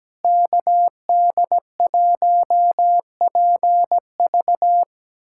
Morse code, heard as a keyed tone, about 17 wpm, 700 Hz. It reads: KD1PV
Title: USA Callsign DE Callsign 17wpm